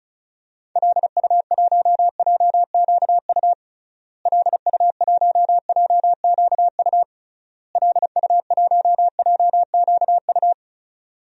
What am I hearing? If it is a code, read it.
Morse code, heard as a keyed tone, about 36 wpm, 700 Hz. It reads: LU1JQU LU1JQU LU1JQU
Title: DXCC-100 Call Signs - 6 Characters - Repeated 3x - 35wpm